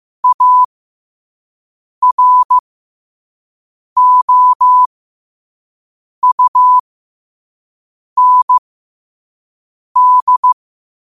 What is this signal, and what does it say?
Morse code, heard as a keyed tone, about 15 wpm, spaced as for 5 wpm, 1000 Hz. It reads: AROUND
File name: qotdfiles / 5wpm